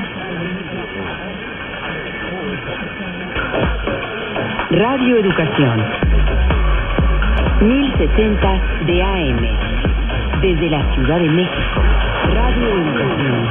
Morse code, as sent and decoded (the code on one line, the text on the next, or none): none